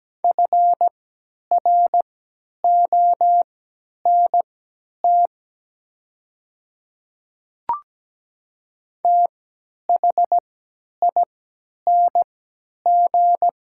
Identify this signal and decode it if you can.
Morse code, heard as a keyed tone, about 17 wpm, spaced as for 10 wpm, 700 Hz. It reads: FRONT THING
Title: 3x Character Spacing - Top 500 Words - 17wpm